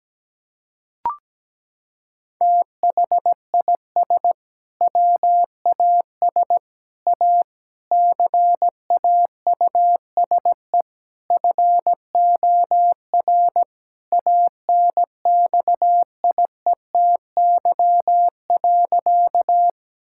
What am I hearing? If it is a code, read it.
Morse code, heard as a keyed tone, about 17 wpm, 700 Hz. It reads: THIS WAS A CAUSE FOR ANXIETY.
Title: Sentences from Top 1600 Words 17wpm